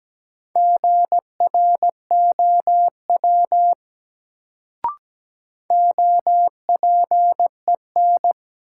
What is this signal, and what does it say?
Morse code, heard as a keyed tone, about 17 wpm, 700 Hz. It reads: GROW OPEN
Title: Sentences from Top 300 Words - New Word Drill - 17wpm